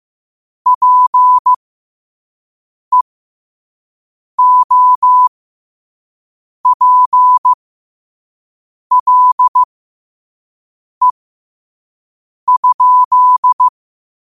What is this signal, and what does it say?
Morse code, heard as a keyed tone, about 15 wpm, spaced as for 5 wpm, 1000 Hz. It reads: PEOPLE?
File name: qotdfiles / 5wpm